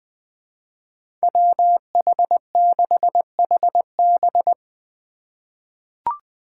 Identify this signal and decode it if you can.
Morse code, heard as a keyed tone, about 20 wpm, 700 Hz. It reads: WH6HB
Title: USA Call Signs - 5 Characters - 8x Character Spacing - 20wpm